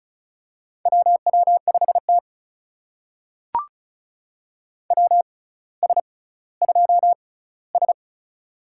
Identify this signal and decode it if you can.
Morse code, heard as a keyed tone, about 36 wpm, spaced as for 18 wpm, 700 Hz. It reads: WW5T WS2S
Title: USA Call Signs - 4 Characters - 6x Character Spacing - 35wpm